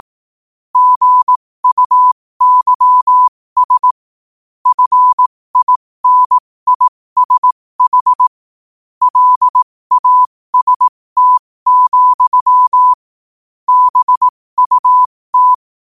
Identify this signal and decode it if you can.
Morse code, heard as a keyed tone, about 18 wpm, spaced as for 15 wpm, 1000 Hz. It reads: GUYS FINISH LAST, BUT
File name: qotdfiles / 15wpm